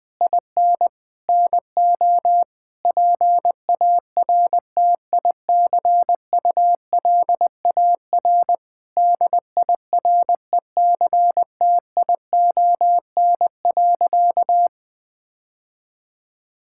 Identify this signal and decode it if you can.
Morse code, heard as a keyed tone, about 20 wpm, 700 Hz. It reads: IN NO PARTICULAR DIRECTION.
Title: Sentences from Top 1000 Words - Review Part 5 - 20wpm